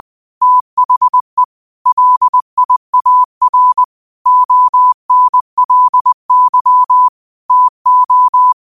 Morse code, heard as a keyed tone, about 20 wpm, 1000 Hz. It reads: THE LIAR ONLY TO
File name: qotdfiles / 20wpm